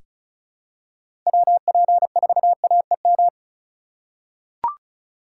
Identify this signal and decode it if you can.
Morse code, heard as a keyed tone, about 36 wpm, 700 Hz. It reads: WP4AEM
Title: USA Call Signs - 2x Character Spacing - 35wpm